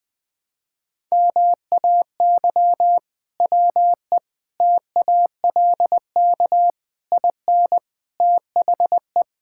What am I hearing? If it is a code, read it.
Morse code, heard as a keyed tone, about 20 wpm, 700 Hz. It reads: MAY WE TALK IN THE
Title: Sentences from Top 1600 Words 20wpm